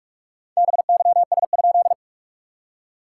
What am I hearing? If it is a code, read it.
Morse code, heard as a keyed tone, about 48 wpm, 660 Hz. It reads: BYS?
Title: SOTA QSO Practice 45wpm